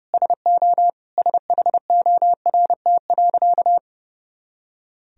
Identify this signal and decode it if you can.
Morse code, heard as a keyed tone, about 30 wpm, 700 Hz. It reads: SO SHORT.
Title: Sentences from Top 500 Words and Abbreviations 30wpm